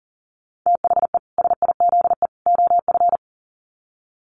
Morse code, heard as a keyed tone, about 40 wpm, 700 Hz. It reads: THE SIZE OF
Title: Sets of 3 Words 40wpm